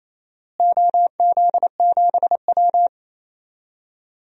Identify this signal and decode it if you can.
Morse code, heard as a keyed tone, about 28 wpm, 700 Hz. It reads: OZ7W
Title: DXCC-100 Call Signs - 4 Characters - Repeated 3x - 28wpm